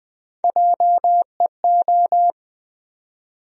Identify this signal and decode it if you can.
Morse code, heard as a keyed tone, about 21 wpm, 700 Hz. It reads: JEO
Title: USA Call Sign Postfixes - 3 Letters 20wpm